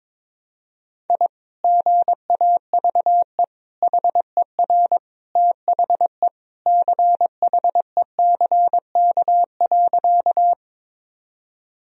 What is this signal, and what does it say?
Morse code, heard as a keyed tone, about 22 wpm, 700 Hz. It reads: I GAVE HER THE CHECK.
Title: Sentences from Top 1000 Words - Review Part 3 - 22wpm